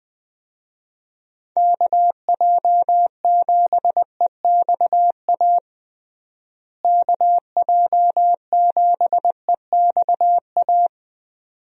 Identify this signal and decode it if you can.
Morse code, heard as a keyed tone, about 21 wpm, 700 Hz. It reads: KJ7EXA KJ7EXA
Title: USA Call Signs - 6 Characters - Repeated 2x - 20wpm